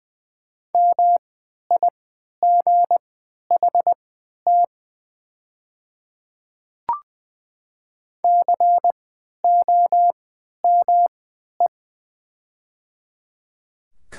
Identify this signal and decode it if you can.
Morse code, heard as a keyed tone, about 20 wpm, spaced as for 11 wpm, 690 Hz. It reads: MIGHT COME
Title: 3x Character Spacing - Top 500 Words - 20wpm